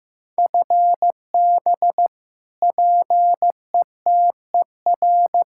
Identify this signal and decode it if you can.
Morse code, heard as a keyed tone, about 15 wpm, 700 Hz. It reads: FB PETER
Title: Practice QSO - Exchange 3 - More Info - 15wpm